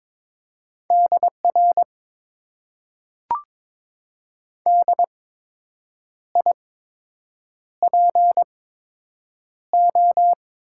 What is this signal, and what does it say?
Morse code, heard as a keyed tone, about 22 wpm, spaced as for 7 wpm, 700 Hz. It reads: DR DIPO
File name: Top 90 QSO Elements - 8x Character Spacing - 22wpm